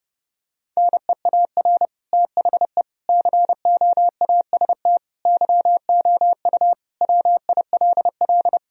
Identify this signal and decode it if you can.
Morse code, heard as a keyed tone, about 30 wpm, 700 Hz. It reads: NEAR THE COAST YOU WILL
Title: Sentences from Top 1000 Words - Review Part 4 - 30wpm